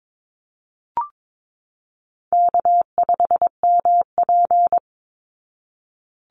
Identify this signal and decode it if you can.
Morse code, heard as a keyed tone, about 22 wpm, 700 Hz. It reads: K5MP
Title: Easy Call Signs Repeated 3x - 22wpm